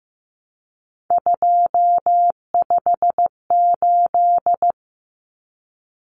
Morse code, heard as a keyed tone, about 15 wpm, 700 Hz. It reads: 258
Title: Numbers - 3-Digit 15wpm